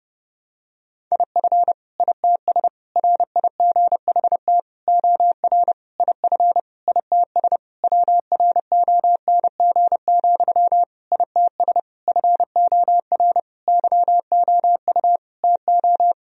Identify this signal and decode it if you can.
Morse code, heard as a keyed tone, about 32 wpm, 700 Hz. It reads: IF ITS RIGHT OR IF ITS WRONG, ITS FOR YOU TO
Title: Sentences from Top 1000 Words - Review Part 3 - 30wpm